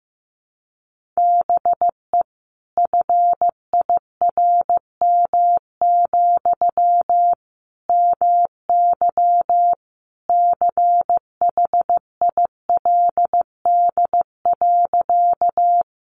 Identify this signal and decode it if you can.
Morse code, heard as a keyed tone, about 15 wpm, 700 Hz. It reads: BE FIRM, MY CHILD.
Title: Sentences from Top 1400 Words 15wpm